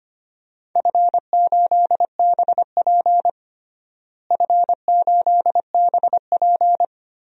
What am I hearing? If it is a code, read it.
Morse code, heard as a keyed tone, about 25 wpm, 700 Hz. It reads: F8BP F8BP
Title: DXCC-100 Call Signs - 4 Characters - Repeated 2x - 25wpm